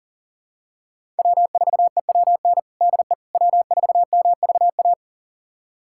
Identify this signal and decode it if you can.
Morse code, heard as a keyed tone, about 40 wpm, 700 Hz. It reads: W4EWN DE W4MVA
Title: USA Callsign DE Callsign - 5 Characters - 40wpm